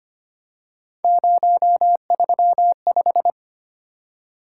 Morse code, heard as a keyed tone, about 25 wpm, 700 Hz. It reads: 035
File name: Three Letter-Number Combinations - 25wpm